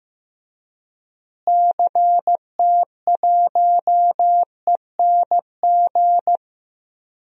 Morse code, heard as a keyed tone, about 15 wpm, 700 Hz. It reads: CT1ENG
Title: DXCC-100 Call Signs - 6 Characters - Repeated 2x - 15wpm